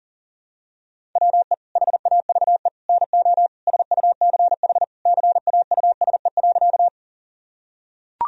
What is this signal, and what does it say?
Morse code, heard as a keyed tone, about 40 wpm, 700 Hz. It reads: WE HAVE NO SUCH CAUSE.
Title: Sentences from Top 1000 Words - Review Part 1 - 40wpm